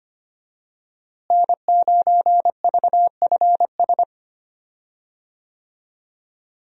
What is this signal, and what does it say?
Morse code, heard as a keyed tone, about 25 wpm, 700 Hz. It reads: N9VFS
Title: USA Call Signs - 5 Characters - Repeated 2x - 25wpm